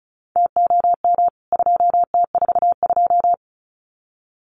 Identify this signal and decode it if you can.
Morse code, heard as a keyed tone, about 34 wpm, 700 Hz. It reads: TOM 2T42
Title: CWT Exchange 35wpm